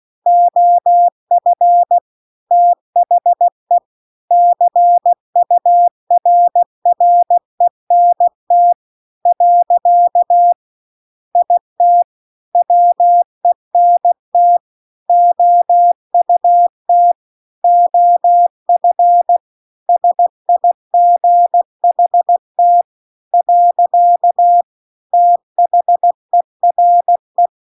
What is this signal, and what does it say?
Morse code, heard as a keyed tone, about 16 wpm, 700 Hz. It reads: OF THE CURRENT . IT WENT OUT OF SIGHT . THERE
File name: CW-The-Big-Twoheaded-River-2-16WPM